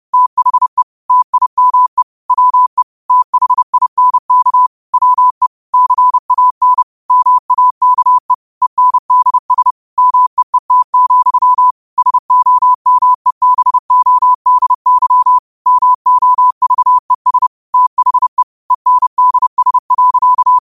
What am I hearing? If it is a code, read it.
Morse code, heard as a keyed tone, about 30 wpm, 1000 Hz. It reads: THE TIME WE THINK WE CAN MAKE ENDS MEET, SOMEBODY MOVES THE ENDS.